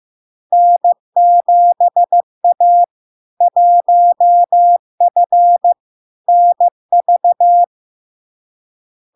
Morse code, heard as a keyed tone, about 15 wpm, 700 Hz. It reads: N7A 1F NV
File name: LARK-FD-2022-Log-in-CW-15wpm-700-Hz